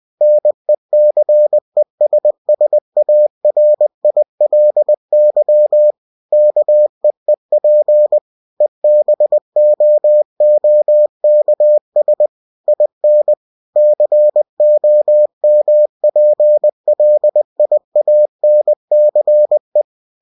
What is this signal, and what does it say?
Morse code, heard as a keyed tone, about 20 wpm, 600 Hz. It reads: NECESSARILY KEEP EBOOKS IN COMPLIANCE
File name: cw-books / winnie-the-pooh / winnie-20wpm-page-0122